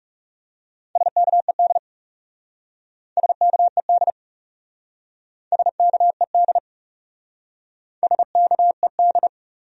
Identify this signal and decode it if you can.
Morse code, heard as a keyed tone, about 36 wpm, 700 Hz. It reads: SKED SKED SKED SKED
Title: Speed-Racing Abbreviations - 30wpm